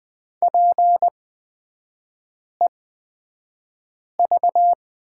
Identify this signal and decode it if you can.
Morse code, heard as a keyed tone, about 20 wpm, spaced as for 5 wpm, 700 Hz. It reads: PEV
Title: 224 Lesson 36e - BT - Three Characters 20wpm